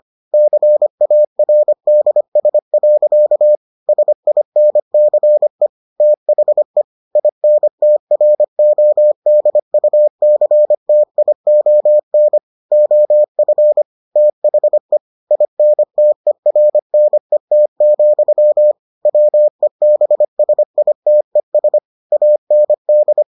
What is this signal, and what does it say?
Morse code, heard as a keyed tone, about 25 wpm, 600 Hz. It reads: CARDS. SINCE THE INTRODUCTION OF THE INTERNET, WEBSITES AND